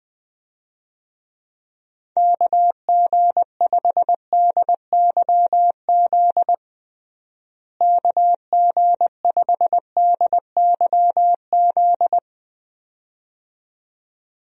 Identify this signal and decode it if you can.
Morse code, heard as a keyed tone, about 21 wpm, 700 Hz. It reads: KG5DYZ KG5DYZ
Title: USA Call Signs - 6 Characters - Repeated 2x - 20wpm